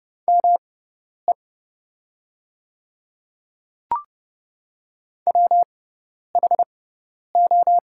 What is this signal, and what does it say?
Morse code, heard as a keyed tone, about 31 wpm, spaced as for 10 wpm, 700 Hz. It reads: ME WHO
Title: 6x Character Spacing - Top 200 Words - 30wpm